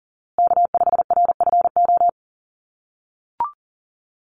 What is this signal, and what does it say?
Morse code, heard as a keyed tone, about 40 wpm, 700 Hz. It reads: K5RFO
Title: USA Call Signs Repeated 2x 40wpm